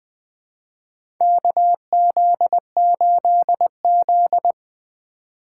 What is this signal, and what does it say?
Morse code, heard as a keyed tone, about 20 wpm, 700 Hz. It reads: KZ8Z
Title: USA Call Signs - 6x Character Spacing - 20wpm